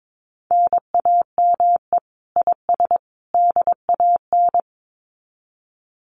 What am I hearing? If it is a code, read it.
Morse code, heard as a keyed tone, about 22 wpm, 700 Hz. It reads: NAME IS DAN
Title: Common QSO Phrases 22wpm